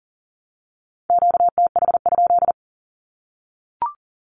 Morse code, heard as a keyed tone, about 40 wpm, 700 Hz. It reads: QTH?
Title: Q-Codes Repeated 3x 40wpm